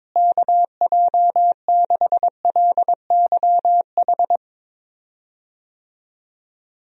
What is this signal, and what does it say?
Morse code, heard as a keyed tone, about 22 wpm, 700 Hz. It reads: KJ6LYH
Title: USA Call Signs - 6 Characters - Repeated 2x - 22wpm